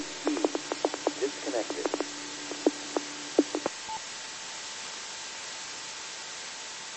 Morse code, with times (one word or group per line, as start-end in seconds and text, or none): none